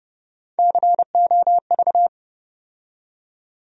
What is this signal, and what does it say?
Morse code, heard as a keyed tone, about 30 wpm, 700 Hz. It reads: COV
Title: USA Call Sign Postfixes - 3 Letters 30wpm